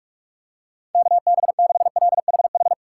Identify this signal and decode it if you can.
Morse code, heard as a keyed tone, about 44 wpm, 700 Hz. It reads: KB6LHH
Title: USA Call Signs - 4x Character Spacing - 45wpm